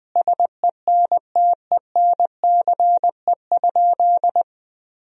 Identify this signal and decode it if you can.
Morse code, heard as a keyed tone, about 21 wpm, 700 Hz. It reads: SENTENCE?
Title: Sentences from Top 1000 Words - Review Part 1 - 20wpm